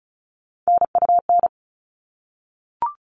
Morse code, heard as a keyed tone, about 34 wpm, 700 Hz. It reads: NUN
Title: All Cut Numbers - 3-Digit 35wpm